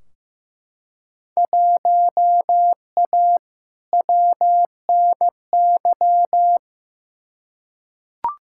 Morse code, heard as a keyed tone, about 15 wpm, 700 Hz. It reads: 1A WNY
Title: Field Day Exchanges - 15wpm